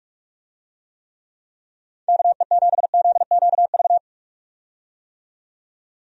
Morse code, heard as a keyed tone, about 46 wpm, 700 Hz. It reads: KE7ZQV
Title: USA Call Signs - 6 Characters - 45wpm